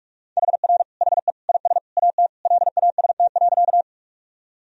Morse code, heard as a keyed tone, about 44 wpm, 700 Hz. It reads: HR HE IS AT LAST.
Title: Sentences from Top 300 Words and Abbreviations 45wpm